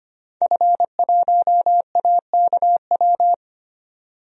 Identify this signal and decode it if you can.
Morse code, heard as a keyed tone, about 25 wpm, 700 Hz. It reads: F1AKW
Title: DXCC-100 Call Signs - 5 Characters - 25wpm